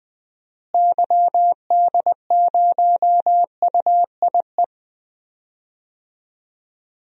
Morse code, heard as a keyed tone, about 20 wpm, 700 Hz. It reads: YD0UIE
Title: DXCC-100 Call Signs - 6 Characters - Repeated 2x - 20wpm